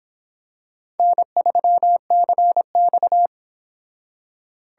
Morse code, heard as a keyed tone, about 26 wpm, 700 Hz. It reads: N3CX